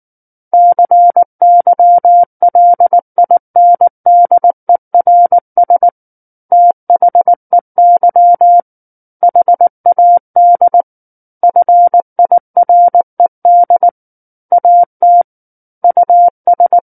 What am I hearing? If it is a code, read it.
Morse code, heard as a keyed tone, about 19 wpm, 700 Hz. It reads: CYLINDERS THEY HAD FIRED AT US
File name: Book One / 10-Chapter 10 - 19 WPM